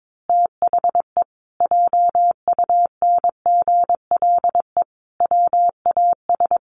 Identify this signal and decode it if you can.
Morse code, heard as a keyed tone, about 22 wpm, 700 Hz. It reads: THE JUNGLE WAS
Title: A Sound Of Thunder 22wpm